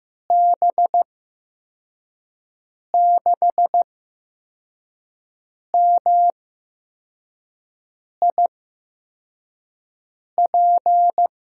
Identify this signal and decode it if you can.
Morse code, heard as a keyed tone, about 15 wpm, spaced as for 4 wpm, 700 Hz. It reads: B6MIP
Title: USA Call Signs - 8x Character Spacing - 15wpm